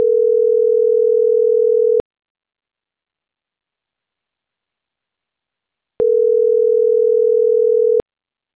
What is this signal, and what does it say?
Ringback tone (North America), 440 + 480 Hz.